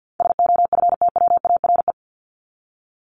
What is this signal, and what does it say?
Morse code, heard as a keyed tone, about 50 wpm, 700 Hz. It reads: SOFTWARE
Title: Top 501-750 Words 50wpm